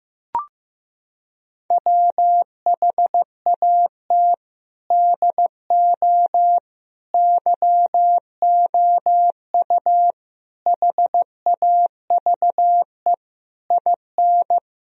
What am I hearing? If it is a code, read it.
Morse code, heard as a keyed tone, about 15 wpm, 700 Hz. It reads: WHAT DO YOU HAVE IN
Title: Sentences from Top 1000 Words - Review Part 3 - 15wpm